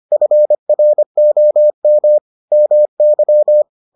Morse code, heard as a keyed tone, about 25 wpm, 600 Hz. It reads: FROM MY